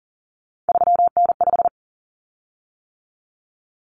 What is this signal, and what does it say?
Morse code, heard as a keyed tone, about 40 wpm, 700 Hz. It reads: 3N5
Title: Common Cut Numbers - 3-Digit 40wpm